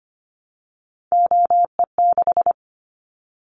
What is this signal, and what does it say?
Morse code, heard as a keyed tone, about 25 wpm, 700 Hz. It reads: OE6
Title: Callsign Prefixes for Top 100 DXCC - 25wpm